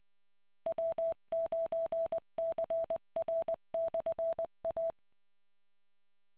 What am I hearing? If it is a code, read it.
Morse code, heard as a keyed tone, about 20 wpm, 660 Hz. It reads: W9CR/A